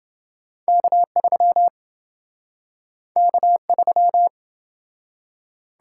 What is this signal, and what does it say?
Morse code, heard as a keyed tone, about 27 wpm, 700 Hz. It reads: K3 K3